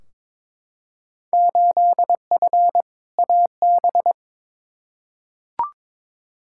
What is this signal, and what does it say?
Morse code, heard as a keyed tone, about 22 wpm, 700 Hz. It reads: 8F AB
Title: Field Day Exchanges Encore - 22wpm